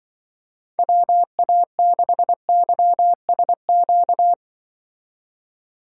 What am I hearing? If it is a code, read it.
Morse code, heard as a keyed tone, about 24 wpm, 700 Hz. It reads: WA6YSQ
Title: Speed Racing - USA 2x3 Call Signs 20wpm